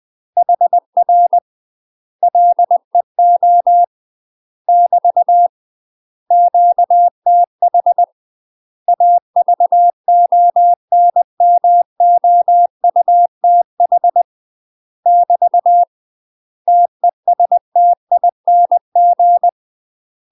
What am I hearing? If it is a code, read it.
Morse code, heard as a keyed tone, about 20 wpm, 700 Hz. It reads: HR LEO <BT> QTH AVONMOUTH <BT> TESTING
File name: UK MASTER QSOs TRAINING / 20WPM / UK MASTER QSOs 0017